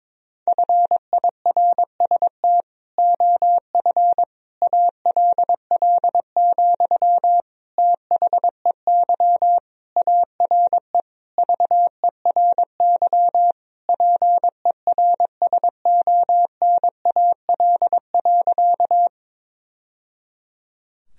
Morse code, heard as a keyed tone, about 22 wpm, 700 Hz. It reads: FIRST OF ALL, THEY ARE VERY PERSONAL.
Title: Sentences from Top 1000 Words - Review Part 4 - 22wpm